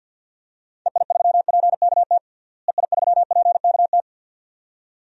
Morse code, heard as a keyed tone, about 50 wpm, 700 Hz. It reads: EI3PKT EI3PKT
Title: DXCC-100 Call Signs - 6 Characters - Repeated 2x - 50wpm